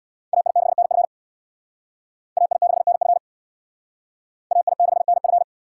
Morse code, heard as a keyed tone, about 62 wpm, 700 Hz. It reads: AI6NF AI6NF AI6NF
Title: Speed Racing - USA 2x2 Call Signs 50wpm